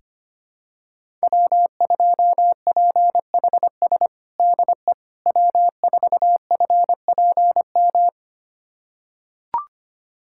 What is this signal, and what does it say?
Morse code, heard as a keyed tone, about 25 wpm, 700 Hz. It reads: W2PHS DE W4FPM
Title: USA Callsign DE Callsign - 5 Characters - 25wpm